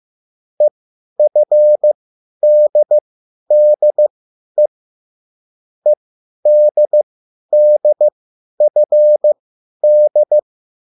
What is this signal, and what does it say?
Morse code, heard as a keyed tone, about 15 wpm, spaced as for 10 wpm, 600 Hz. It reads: EFDDE EDDFD